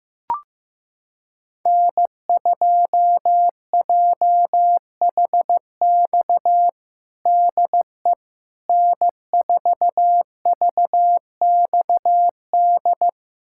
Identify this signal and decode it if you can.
Morse code, heard as a keyed tone, about 15 wpm, 700 Hz. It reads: N2JHX DE N4VXD
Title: USA Callsign DE Callsign - 5 Characters - 15wpm